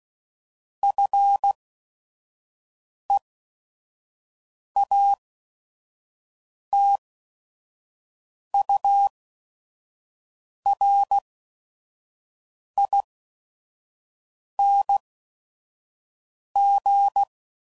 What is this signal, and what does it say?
Morse code, heard as a keyed tone, about 16 wpm, spaced as for 5 wpm, 800 Hz. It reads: FEATURING
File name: Suffixes / Special ING's / VerySlowToFaster ING's